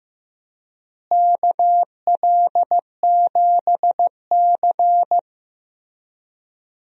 Morse code, heard as a keyed tone, about 15 wpm, 700 Hz. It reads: KL7C
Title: DXCC-100 Call Signs - 4 Characters - Repeated 3x - 15wpm